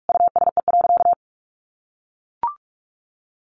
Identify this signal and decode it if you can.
Morse code, heard as a keyed tone, about 44 wpm, 700 Hz. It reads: USE.
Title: Sentences from Top 1400 Words 45wpm